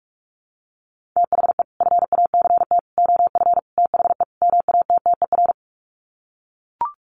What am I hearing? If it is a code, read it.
Morse code, heard as a keyed tone, about 44 wpm, 700 Hz. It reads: THE FACT OF THE MATTER
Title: Sets of 5 Words 45wpm